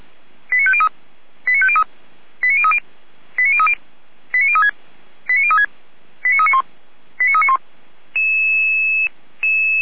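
Five-tone selective call (ZVEI), heard as quick runs of five tones, tones of about 70 ms. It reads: 89503 89503 89030 89030 89036 89036 89391 89391
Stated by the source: ZVEI five-tone selective call 89042 received by SDR.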